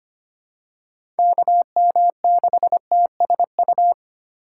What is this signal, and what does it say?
Morse code, heard as a keyed tone, about 25 wpm, 700 Hz. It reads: KM6TSU
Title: USA Call Signs - 6 Characters - 4x Character Spacing - 25wpm